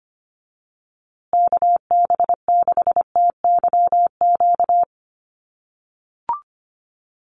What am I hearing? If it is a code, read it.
Morse code, heard as a keyed tone, about 25 wpm, 700 Hz. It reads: KB6TYQ
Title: USA Call Signs Repeated 2x 25wpm